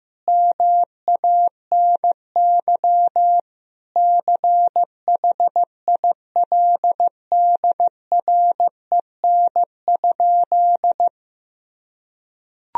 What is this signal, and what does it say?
Morse code, heard as a keyed tone, about 15 wpm, 700 Hz. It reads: MANY CHILDREN?
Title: Sentences from Top 300 Words and Abbreviations 15wpm